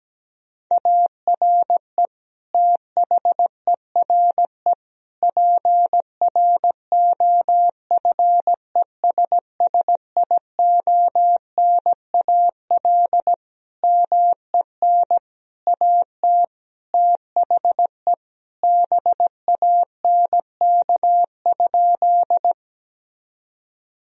Morse code, heard as a keyed tone, about 17 wpm, 700 Hz. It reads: ARE THERE PROFESSIONAL MEN AT THE BANK?
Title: Sentences from Top 1000 Words - Review Part 4 - 17wpm